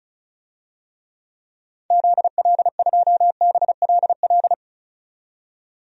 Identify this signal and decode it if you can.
Morse code, heard as a keyed tone, about 34 wpm, 700 Hz. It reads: ZL2BLL
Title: DXCC-100 Call Signs - 6 Characters - 35wpm